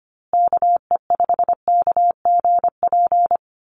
Morse code, heard as a keyed tone, about 25 wpm, 700 Hz. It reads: KE5KGP
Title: USA Call Signs Repeated 3x 25wpm